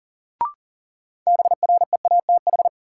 Morse code, heard as a keyed tone, about 40 wpm, 700 Hz. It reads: BREATH
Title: Sets of 5 Words Encore - New Word Drill - 40wpm